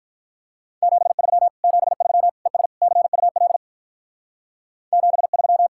Morse code, heard as a keyed tone, about 44 wpm, 700 Hz. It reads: 73 73 ES CUL 73